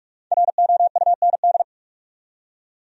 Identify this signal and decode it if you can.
Morse code, heard as a keyed tone, about 55 wpm, 700 Hz. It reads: ROUND
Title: Speed-Racing Top 200 Words 45wpm